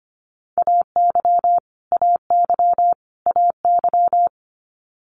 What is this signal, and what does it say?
Morse code, heard as a keyed tone, about 25 wpm, 700 Hz. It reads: AY AY AY
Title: Multi-letter Phonemes Repeated 3x 25wpm